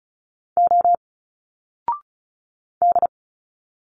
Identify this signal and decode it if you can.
Morse code, heard as a keyed tone, about 36 wpm, 700 Hz. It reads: O D
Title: Single Letters 35wpm